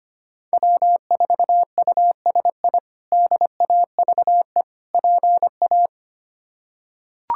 Morse code, heard as a keyed tone, about 25 wpm, 700 Hz. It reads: W4USI DAVE PA
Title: K1USN SST QSO Practice - 25wpm